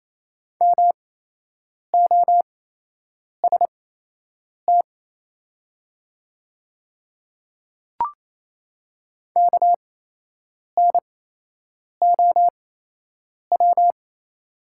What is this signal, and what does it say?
Morse code, heard as a keyed tone, about 28 wpm, spaced as for 8 wpm, 700 Hz. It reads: MOST KNOW
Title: 8x Character Spacing - Top 100 Words - 28wpm